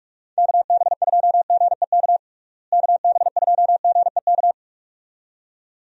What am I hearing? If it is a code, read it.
Morse code, heard as a keyed tone, about 46 wpm, 700 Hz. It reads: KB2GEK KB2GEK
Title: USA Call Signs - 6 Characters - Repeated 2x - 45wpm